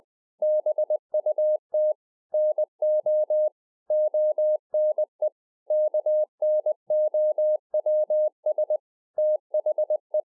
Morse code, heard as a keyed tone, about 20 wpm, 610 Hz. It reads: BUT NO ONE KNOWS THE
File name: 20WPM